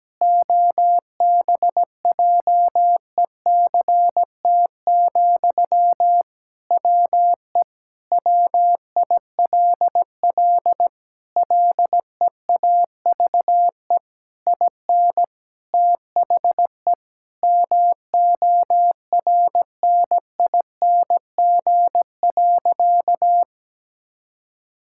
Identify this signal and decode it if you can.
Morse code, heard as a keyed tone, about 17 wpm, 700 Hz. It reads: OBJECT, WE WILL LEAVE IN THE MORNING.
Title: Sentences from Top 1000 Words - Review Part 6 - 17wpm